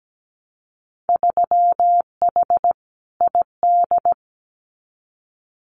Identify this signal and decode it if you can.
Morse code, heard as a keyed tone, about 17 wpm, 700 Hz. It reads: 3H ID
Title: Winter Field Day Exchanges Encore 17wpm